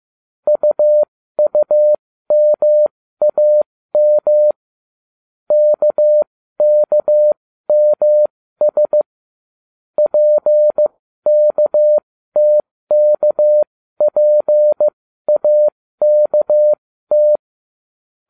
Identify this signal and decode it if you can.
Morse code, heard as a keyed tone, about 15 wpm, spaced as for 13 wpm, 600 Hz. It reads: UUMAM KKMS PKTKPAKT
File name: cw / ubacw07 1